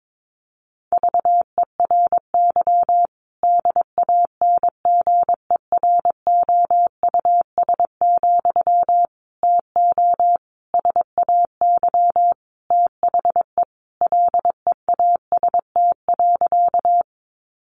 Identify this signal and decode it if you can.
Morse code, heard as a keyed tone, about 22 wpm, 700 Hz. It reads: VERY DANGEROUS, TO SAY THE LEAST.
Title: Sentences from Top 1200 Words 22wpm